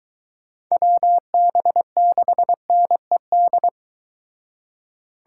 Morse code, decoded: WB6NED

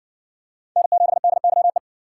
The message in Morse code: N7DYE

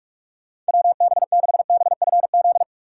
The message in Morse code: WB6BFZ

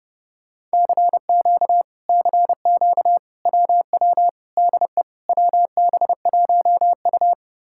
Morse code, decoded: CQ CQ WW DE WB1U

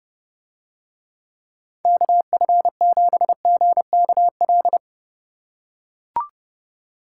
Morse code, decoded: KF7GKL

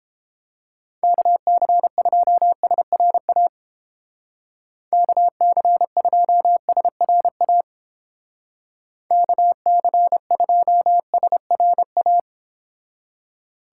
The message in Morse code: KC2SRA KC2SRA KC2SRA

